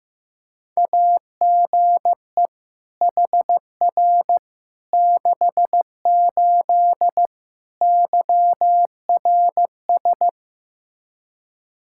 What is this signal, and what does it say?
Morse code, heard as a keyed tone, about 16 wpm, 700 Hz. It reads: AGE HR 68 YRS
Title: K1USN SST QSO Practice 15wpm